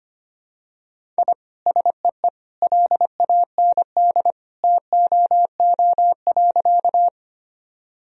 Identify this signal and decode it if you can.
Morse code, heard as a keyed tone, about 25 wpm, 700 Hz. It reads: I SEE LAND TOO.
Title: Sentences from Top 1000 Words - Review Part 1 - 25wpm